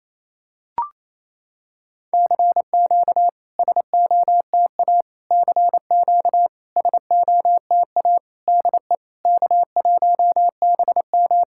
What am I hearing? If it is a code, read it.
Morse code, heard as a keyed tone, about 28 wpm, 690 Hz. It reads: CQ SOTA CQ SOTA DE K1BM
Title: SOTA QSO Practice 28wpm